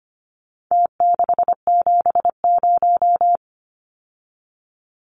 Morse code, heard as a keyed tone, about 25 wpm, 700 Hz. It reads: T670
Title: Numbers - Contest Style - 4-Digit 25wpm